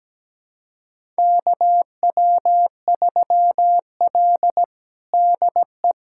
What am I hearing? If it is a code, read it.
Morse code, heard as a keyed tone, about 17 wpm, 700 Hz. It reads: KW3L DE
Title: USA Callsign DE Callsign - 4 Characters - 17wpm